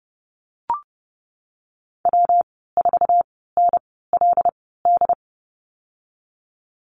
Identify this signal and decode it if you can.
Morse code, heard as a keyed tone, about 30 wpm, spaced as for 17 wpm, 700 Hz. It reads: W4NLD
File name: course / 30:10wpm / 116 Lesson 19g - Letter w - Call Signs 30wpm